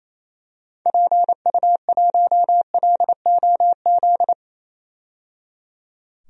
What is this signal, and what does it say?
Morse code, heard as a keyed tone, about 27 wpm, 700 Hz. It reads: PU1LOZ
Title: DXCC-100 Call Signs - 6 Characters - 28wpm